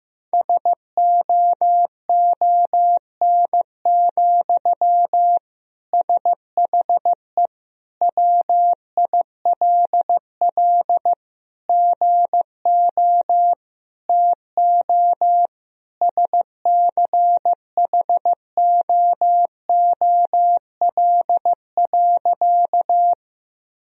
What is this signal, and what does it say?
Morse code, heard as a keyed tone, about 15 wpm, 700 Hz. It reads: SOON, SHE WILL GO TO SCHOOL.
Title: Sentences from Top 1000 Words - Review Part 1 - 15wpm